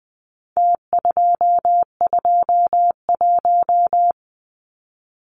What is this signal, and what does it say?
Morse code, heard as a keyed tone, about 20 wpm, 700 Hz. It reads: T221
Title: Numbers - Contest Style - 4-Digit 20wpm